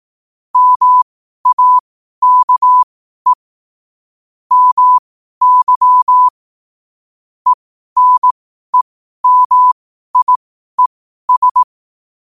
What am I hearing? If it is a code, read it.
Morse code, heard as a keyed tone, about 18 wpm, spaced as for 12 wpm, 1000 Hz. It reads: MAKE MY ENEMIES
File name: qotdfiles / 12wpm